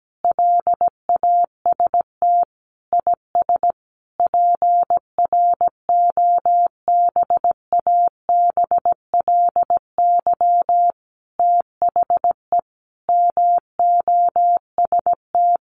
Morse code, heard as a keyed tone, about 17 wpm, 700 Hz. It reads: LAST IS PROBABLY THE MOST
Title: Sentences from Top 700 Words 17wpm